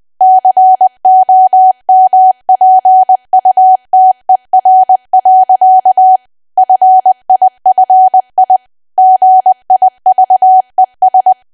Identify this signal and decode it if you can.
Morse code, heard as a keyed tone, about 20 wpm, 750 Hz. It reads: COMPUTER. FIFI GIVES
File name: cwa / W1AW-Level3 / 20WPM